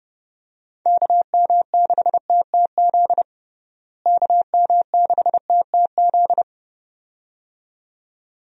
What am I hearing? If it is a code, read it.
Morse code, heard as a keyed tone, about 32 wpm, 700 Hz. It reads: KM6TTZ KM6TTZ
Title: USA Call Signs - 6 Characters - Repeated 2x - 30wpm